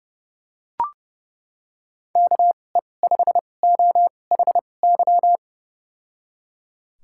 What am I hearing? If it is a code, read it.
Morse code, heard as a keyed tone, about 32 wpm, spaced as for 22 wpm, 700 Hz. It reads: KE5OHY
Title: USA Call Signs - 6 Characters - 2x Character Spacing - 30wpm